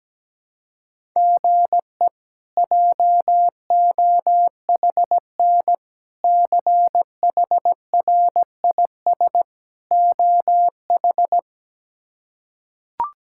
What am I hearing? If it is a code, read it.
Morse code, heard as a keyed tone, about 17 wpm, 700 Hz. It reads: GE JOHN CHRIS OH
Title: K1USN SST QSO Practice - 17wpm